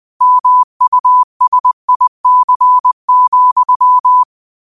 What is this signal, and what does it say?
Morse code, heard as a keyed tone, about 20 wpm, 1000 Hz. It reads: MUSIC,
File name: 20wpm